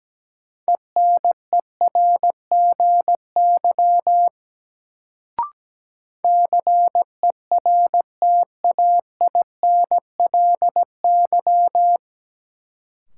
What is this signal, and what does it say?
Morse code, heard as a keyed tone, about 17 wpm, 700 Hz. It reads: ENERGY CERTAINLY
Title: Sentences from Top 900 Words - New Word Drill - 17wpm